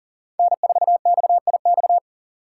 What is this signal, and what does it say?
Morse code, heard as a keyed tone, about 38 wpm, 700 Hz. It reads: N4XIX
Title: USA Call Signs - 2x Character Spacing - 40wpm